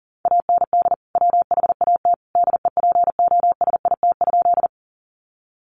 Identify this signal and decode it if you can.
Morse code, heard as a keyed tone, about 40 wpm, 700 Hz. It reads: AND WHAT DEPOSIT?
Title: Sentences from Top 1300 Words 40wpm